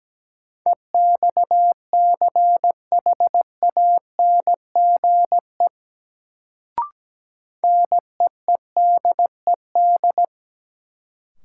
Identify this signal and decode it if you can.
Morse code, heard as a keyed tone, about 17 wpm, 700 Hz. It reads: EXCHANGE NEEDED
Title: Sentences from Top 1000 Words - Review Part 6 - New Word Drill - 17wpm